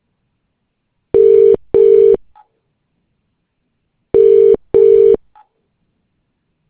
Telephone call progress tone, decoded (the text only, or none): ringback tone